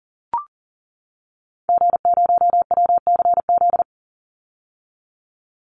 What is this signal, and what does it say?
Morse code, heard as a keyed tone, about 40 wpm, 700 Hz. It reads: G0WCZ
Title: Call Signs - 40wpm